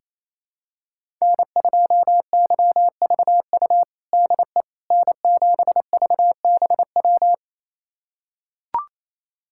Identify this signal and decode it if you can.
Morse code, heard as a keyed tone, about 28 wpm, 700 Hz. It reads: N2YVU DE N7VBW
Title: USA Callsign DE Callsign 28wpm